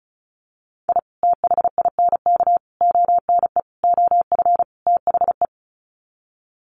Morse code, heard as a keyed tone, about 34 wpm, 700 Hz. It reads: I THINK ONE OF THE
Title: Sets of 5 Words 35wpm